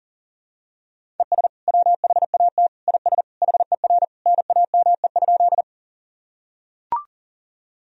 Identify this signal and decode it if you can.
Morse code, heard as a keyed tone, about 40 wpm, 700 Hz. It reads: ES WHAT IS HER NAME?
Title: Sentences from Top 200 Words and Abbreviations 40wpm